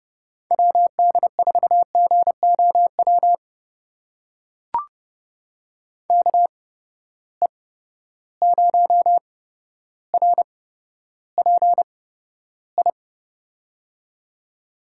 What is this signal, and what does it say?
Morse code, heard as a keyed tone, about 30 wpm, spaced as for 13 wpm, 700 Hz. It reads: WD4GOW KE0RPI
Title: USA Call Signs - 6 Characters - 8x Character Spacing - 30wpm